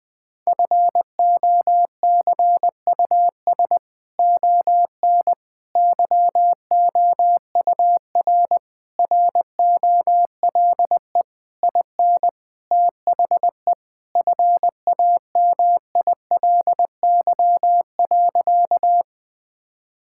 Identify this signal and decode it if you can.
Morse code, heard as a keyed tone, about 20 wpm, 700 Hz. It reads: FOCUS ON YOUR ROLE IN THE FAMILY.
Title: Sentences from Top 1000 Words - Review Part 6 - 20wpm